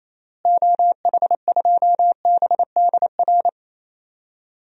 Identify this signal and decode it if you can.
Morse code, heard as a keyed tone, about 27 wpm, 700 Hz. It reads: OH2BDR
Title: DXCC-100 Call Signs - 6 Characters - Repeated 2x - 28wpm